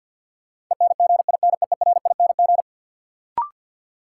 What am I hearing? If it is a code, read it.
Morse code, heard as a keyed tone, about 53 wpm, 700 Hz. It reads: ENGINEERING
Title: Sentences from Top 1500 Words - New Word Drill Alt - 50wpm